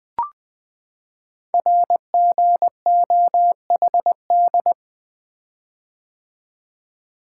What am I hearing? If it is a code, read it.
Morse code, heard as a keyed tone, about 20 wpm, 700 Hz. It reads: RGOHD
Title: Code Groups - 5 Letters - 20wpm